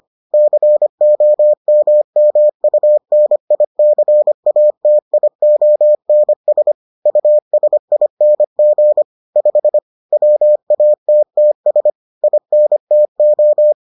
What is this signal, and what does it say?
Morse code, heard as a keyed tone, about 25 wpm, 600 Hz. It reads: COMMUNICATIONS USING 5 WATTS INTO